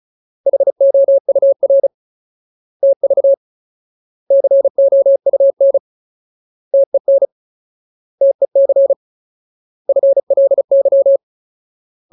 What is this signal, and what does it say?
Morse code, heard as a keyed tone, about 34 wpm, 550 Hz. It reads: HOUR TV COUN TEN TEC FLY
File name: Ultimate Word List / 35WPM Ultimate